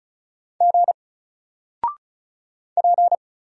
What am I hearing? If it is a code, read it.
Morse code, heard as a keyed tone, about 35 wpm, 700 Hz. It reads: G P
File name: Single-Letter-Number-35wpm